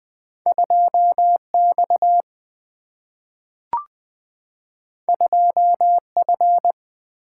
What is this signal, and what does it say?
Morse code, heard as a keyed tone, about 20 wpm, 700 Hz. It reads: 2X 2F
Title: Single-Letter Single-Number 20wpm